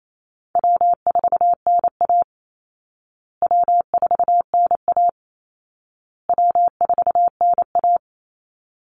Morse code, heard as a keyed tone, about 28 wpm, 700 Hz. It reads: W4NA W4NA W4NA
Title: Easy Call Signs Repeated 3x - 28wpm